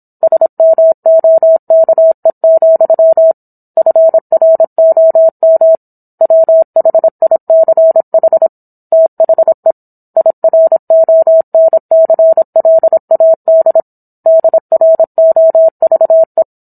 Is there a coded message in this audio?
Yes: Morse code from 0.2 to 16.4 s, heard as a keyed tone, about 27 wpm, 650 Hz. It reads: SMOKE, FROM WHICH THE IRONCLAD DROVE